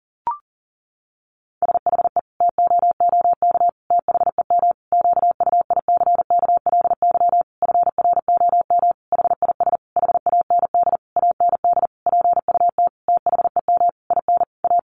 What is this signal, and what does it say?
Morse code, heard as a keyed tone, about 37 wpm, 700 Hz. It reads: SHE TOOK THEM QUICKLY FROM HIS HAND AND PUT THEM IN A
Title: Sentences from Top 1300 Words 40wpm